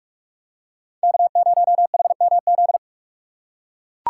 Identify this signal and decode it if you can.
Morse code, heard as a keyed tone, about 44 wpm, 700 Hz. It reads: K0HMZ
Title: K1USN SST QSO Practice - 45wpm